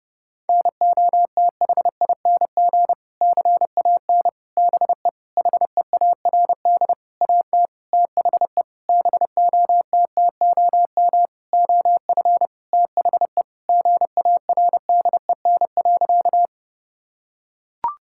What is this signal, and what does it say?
Morse code, heard as a keyed tone, about 32 wpm, 700 Hz. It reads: NOTHING CAN BE HEARD AT THE BOTTOM OF THE GARDEN.
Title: Sentences from Top 1000 Words - Review Part 6 - 30wpm